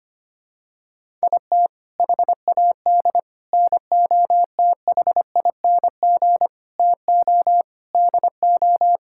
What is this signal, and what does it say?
Morse code, heard as a keyed tone, about 25 wpm, 700 Hz. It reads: IT HAD NOTHING TO DO
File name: practice / 25wpm / Sentences with 4-Word Phrases - 25wpm